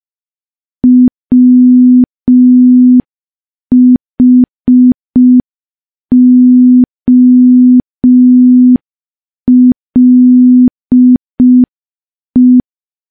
Morse code, heard as a keyed tone, about 5 wpm, 250 Hz. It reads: WHOLE
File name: morse / morse-training-05WPM